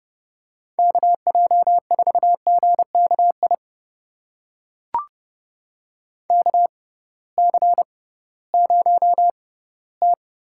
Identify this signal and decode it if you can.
Morse code, heard as a keyed tone, about 32 wpm, spaced as for 18 wpm, 700 Hz. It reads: KJ4GKI KC0T
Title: USA Call Signs - 6 Characters - 6x Character Spacing - 30wpm